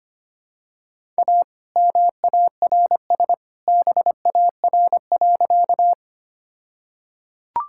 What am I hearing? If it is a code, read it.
Morse code, heard as a keyed tone, about 25 wpm, 700 Hz. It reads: A MARS BAR.
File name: practice / 25wpm / All-Age Appropriate Jokes - 25wpm